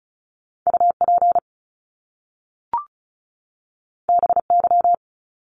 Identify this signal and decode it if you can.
Morse code, heard as a keyed tone, about 36 wpm, 700 Hz. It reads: UP BY
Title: Top Two Letter Combinations in Words 35wpm